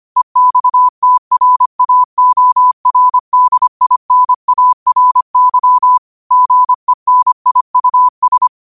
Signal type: Morse code